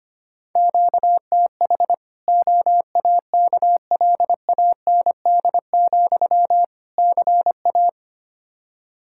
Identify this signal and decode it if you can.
Morse code, heard as a keyed tone, about 25 wpm, 700 Hz. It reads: QTH OAKLAND, CA
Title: K1USN SST QSO Practice 25wpm